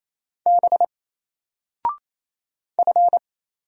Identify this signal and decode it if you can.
Morse code, heard as a keyed tone, about 27 wpm, 700 Hz. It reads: B F
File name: Single-Letter-Number-28wpm